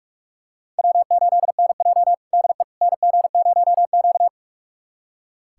Morse code, heard as a keyed tone, about 46 wpm, 700 Hz. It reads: W8NJ DE NG0Q
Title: USA Callsign DE Callsign - 4 Characters - 45wpm